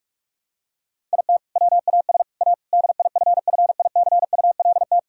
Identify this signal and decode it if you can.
Morse code, heard as a keyed tone, about 41 wpm, 700 Hz. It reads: IT WAS A DIFFICULT
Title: Sentences from Top 1000 Words - Review Part 6 - 45wpm